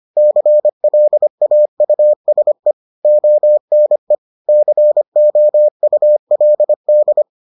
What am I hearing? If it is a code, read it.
Morse code, heard as a keyed tone, about 25 wpm, 600 Hz. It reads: CLAUSE ONE COULD